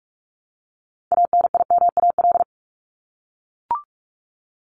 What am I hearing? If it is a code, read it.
Morse code, heard as a keyed tone, about 44 wpm, 700 Hz. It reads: ANIMAL
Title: ES:S Suffix - Top 100 Words with Base Words 45wpm